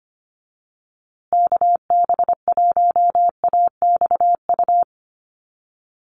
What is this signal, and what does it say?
Morse code, heard as a keyed tone, about 25 wpm, 700 Hz. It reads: KB1AXU
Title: USA Call Signs Repeated 3x 25wpm